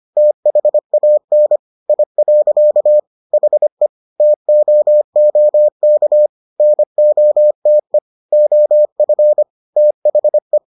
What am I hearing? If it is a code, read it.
Morse code, heard as a keyed tone, about 25 wpm, 600 Hz. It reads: THAN I. HE TOOK NOTE OF THE